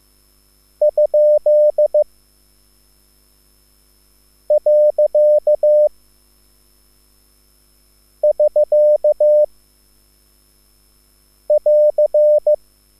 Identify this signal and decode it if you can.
Morse code, heard as a keyed tone, about 15 wpm, 600 Hz. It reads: ? . <SK> <AR>
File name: QSL / CW lesson